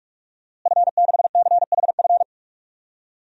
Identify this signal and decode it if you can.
Morse code, heard as a keyed tone, about 41 wpm, 700 Hz. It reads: F6CHF
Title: DXCC-100 Call Signs - 5 Characters - Repeated 3x - 45wpm